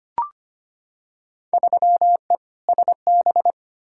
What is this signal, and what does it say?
Morse code, heard as a keyed tone, about 25 wpm, 700 Hz. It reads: 3E SB
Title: Field Day Exchanges Encore - 25wpm